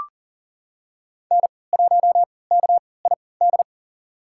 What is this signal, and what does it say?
Morse code, heard as a keyed tone, about 40 wpm, spaced as for 23 wpm, 700 Hz. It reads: N1KID
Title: USA Call Signs - 5 Characters - 3x Character Spacing - 40wpm